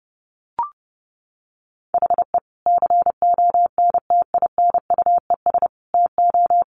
Morse code, heard as a keyed tone, about 30 wpm, 700 Hz. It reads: HE CONTINUES TO
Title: Easy Phrases 30wpm